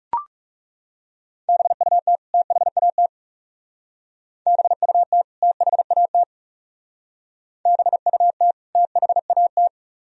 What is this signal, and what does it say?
Morse code, heard as a keyed tone, about 34 wpm, 700 Hz. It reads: BUT THAT BUT THAT BUT THAT
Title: Speed-Racing - Sets of 2 Words 30wpm